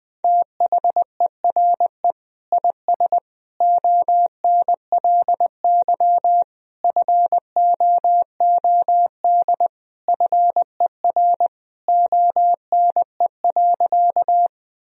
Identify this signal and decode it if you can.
Morse code, heard as a keyed tone, about 20 wpm, 700 Hz. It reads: THERE IS ONLY FOOD FER ONE.
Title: Sentences from Top 300 Words and Abbreviations 20wpm